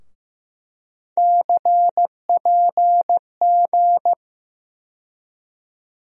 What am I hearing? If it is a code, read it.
Morse code, heard as a keyed tone, about 15 wpm, 700 Hz. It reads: CPG
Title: USA Call Sign Postfixes - 3 Letters 15wpm